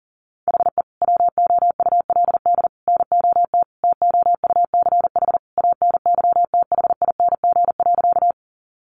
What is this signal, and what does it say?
Morse code, heard as a keyed tone, about 40 wpm, 700 Hz. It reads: HE WOULD NOT TOUCH ANYTHING.
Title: Sentences from Top 1200 Words 40wpm